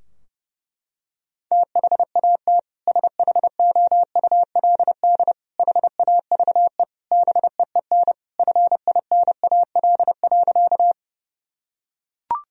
Morse code, heard as a keyed tone, about 30 wpm, 700 Hz. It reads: THAT SHOULD HAVE BEEN FINAL.
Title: Sentences from Top 1000 Words - Review Part 2 - 30wpm